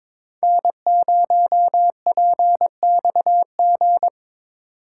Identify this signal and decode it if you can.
Morse code, heard as a keyed tone, about 26 wpm, 700 Hz. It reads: N0PXG
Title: Speed Racing - USA 1x3 Call Signs 22wpm